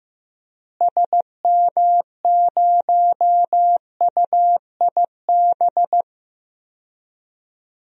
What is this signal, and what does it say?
Morse code, heard as a keyed tone, about 15 wpm, 700 Hz. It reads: SM0UIB
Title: DXCC-100 Call Signs - 6 Characters - Repeated 3x - 15wpm